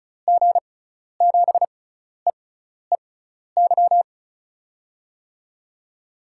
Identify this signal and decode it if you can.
Morse code, heard as a keyed tone, about 36 wpm, spaced as for 12 wpm, 700 Hz. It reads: G7EEY
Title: USA Call Signs - 6 Characters - 6x Character Spacing - 35wpm